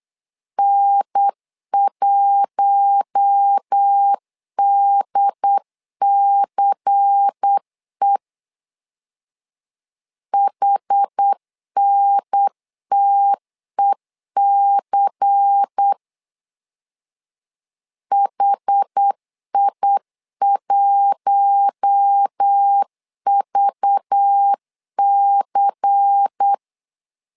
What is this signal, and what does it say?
Morse code, heard as a keyed tone, about 8 wpm, 800 Hz. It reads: N1DCE HNTEC HI1VC
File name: LEZIONI / 5 LBVC / 5 LBVC 8wpm 2935